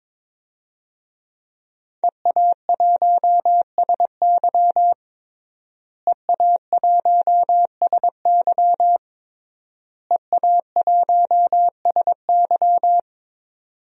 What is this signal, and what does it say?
Morse code, heard as a keyed tone, about 22 wpm, 700 Hz. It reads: EA1SY EA1SY EA1SY
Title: DXCC-100 Call Signs - 5 Characters - Repeated 3x - 22wpm